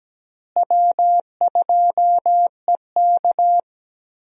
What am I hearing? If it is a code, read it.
Morse code, heard as a keyed tone, about 17 wpm, 700 Hz. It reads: W2EK
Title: USA Call Signs - 4 Characters - Repeated 2x - 17wpm